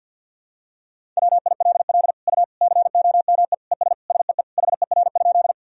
Morse code, heard as a keyed tone, about 47 wpm, 700 Hz. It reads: WILL U COME ES SEE HER?
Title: Sentences from Top 100 Words and Abbreviations 50wpm